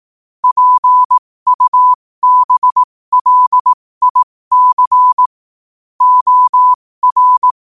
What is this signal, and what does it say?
Morse code, heard as a keyed tone, about 18 wpm, spaced as for 15 wpm, 1000 Hz. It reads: PUBLIC OR